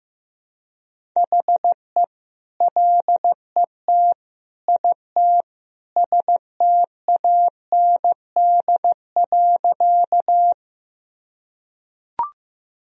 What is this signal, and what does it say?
Morse code, heard as a keyed tone, about 15 wpm, 700 Hz. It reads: HE LET IT STAND.
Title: Sentences from Top 1000 Words - Review Part 1 - 15wpm